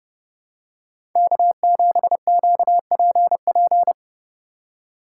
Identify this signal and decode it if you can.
Morse code, heard as a keyed tone, about 27 wpm, 700 Hz. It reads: K7QPP